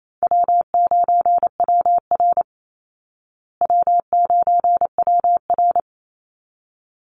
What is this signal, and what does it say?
Morse code, heard as a keyed tone, about 28 wpm, 700 Hz. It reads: W9WR W9WR
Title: Easy Call Signs Repeated 2x - 28wpm